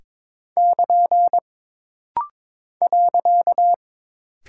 Morse code, heard as a keyed tone, about 22 wpm, 700 Hz. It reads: <KN> .
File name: practice / 22wpm / Single Punctuation-Prosign - Rapid-Fire - 22wpm